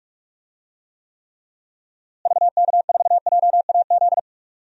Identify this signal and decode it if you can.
Morse code, heard as a keyed tone, about 46 wpm, 700 Hz. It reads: VK4JAZ
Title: DXCC-100 Call Signs - 6 Characters - Repeated 3x - 45wpm